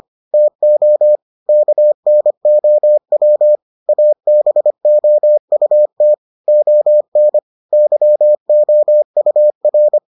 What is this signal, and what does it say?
Morse code, heard as a keyed tone, about 25 wpm, 600 Hz. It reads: TO KNOW ABOUT ON YOUR